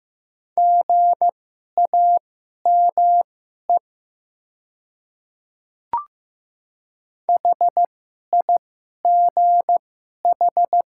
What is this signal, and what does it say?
Morse code, heard as a keyed tone, about 15 wpm, spaced as for 11 wpm, 700 Hz. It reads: GAME HIGH
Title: 2x Character Spacing - Top 500 Words - 15wpm